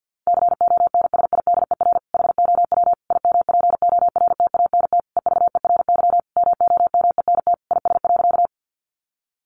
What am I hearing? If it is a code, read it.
Morse code, heard as a keyed tone, about 50 wpm, 700 Hz. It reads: CONSIDER HOW IMPORTANT EVERY MOMENT IS.
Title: Sentences from Top 1000 Words 50wpm